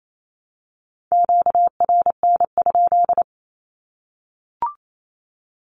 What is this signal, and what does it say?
Morse code, heard as a keyed tone, about 28 wpm, 700 Hz. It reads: QRN?
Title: Q-Codes Repeated 3x 28wpm